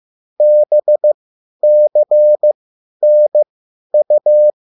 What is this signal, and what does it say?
Morse code, heard as a keyed tone, about 15 wpm, spaced as for 10 wpm, 600 Hz. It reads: BCNU